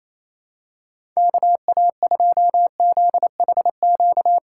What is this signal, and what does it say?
Morse code, heard as a keyed tone, about 28 wpm, 700 Hz. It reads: KA2ZHQ
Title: USA Call Signs - 6 Characters - 6x Character Spacing - 28wpm